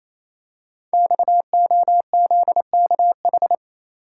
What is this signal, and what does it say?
Morse code, heard as a keyed tone, about 29 wpm, 700 Hz. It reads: XOZKH
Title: Code Groups - 5 Letter:Numbers - 28wpm